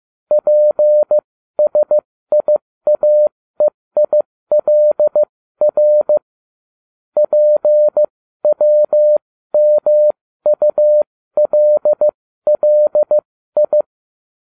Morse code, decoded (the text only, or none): PSIAEILR PWMULLI